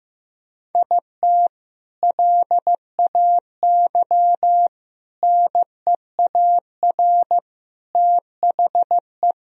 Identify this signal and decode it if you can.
Morse code, heard as a keyed tone, about 15 wpm, 700 Hz. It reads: IT LAY NEAR THE
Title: Sentences from Top 1000 Words - Review Part 2 - 15wpm